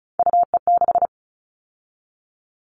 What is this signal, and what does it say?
Morse code, heard as a keyed tone, about 34 wpm, 700 Hz. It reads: UE6
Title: All Cut Numbers - 3-Digit 35wpm